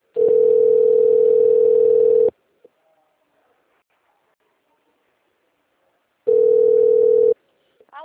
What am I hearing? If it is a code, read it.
Ringback tone (North America), 440 + 480 Hz, with speech.
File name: rg-1600-unknown- - - .13634